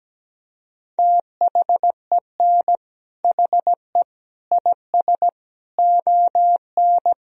Morse code, heard as a keyed tone, about 17 wpm, 700 Hz. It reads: THEN HE IS ON
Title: Sentences from Top 1000 Words - Review Part 4 - 17wpm